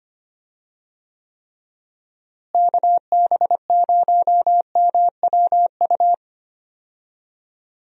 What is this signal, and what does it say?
Morse code, heard as a keyed tone, about 25 wpm, 700 Hz. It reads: KB0MWU
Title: USA Call Signs - 6 Characters - Repeated 3x - 25wpm